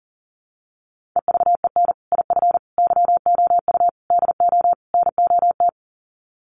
Morse code, heard as a keyed tone, about 40 wpm, 700 Hz. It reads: EVEN IF YOU DO NOT
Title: Sets of 5 Words Encore 40wpm